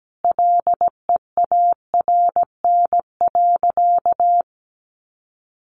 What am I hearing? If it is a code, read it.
Morse code, heard as a keyed tone, about 17 wpm, 700 Hz. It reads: LEARN.
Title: Sentences from Top 800 Words 17wpm